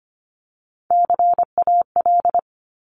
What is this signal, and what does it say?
Morse code, heard as a keyed tone, about 25 wpm, 700 Hz. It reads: CAL
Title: Top Three Letter Combinations in Words 25wpm